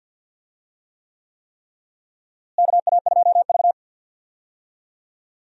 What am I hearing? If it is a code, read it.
Morse code, heard as a keyed tone, about 53 wpm, 700 Hz. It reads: KA2V
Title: USA Call Signs - 4 Characters - Repeated 3x - 50wpm